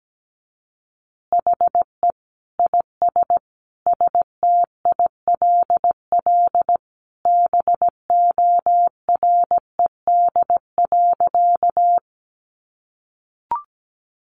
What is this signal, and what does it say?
Morse code, heard as a keyed tone, about 17 wpm, 700 Hz. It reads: HE IS STILL BORED.
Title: Easy Phrases 17wpm